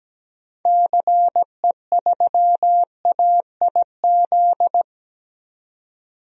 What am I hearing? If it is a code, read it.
Morse code, heard as a keyed tone, about 17 wpm, 700 Hz. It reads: CE3AIZ
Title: DXCC-100 Call Signs - 6 Characters - 17wpm